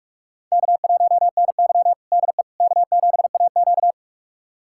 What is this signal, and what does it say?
Morse code, heard as a keyed tone, about 43 wpm, 700 Hz. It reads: K1NY DE K7AQ
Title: USA Callsign DE Callsign - 4 Characters - 45wpm